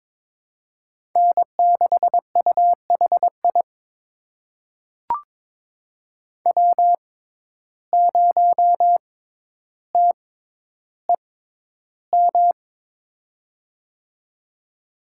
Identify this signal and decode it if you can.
Morse code, heard as a keyed tone, about 22 wpm, spaced as for 11 wpm, 700 Hz. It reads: N6UHI W0TEM
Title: USA Call Signs - 5 Characters - 6x Character Spacing - 22wpm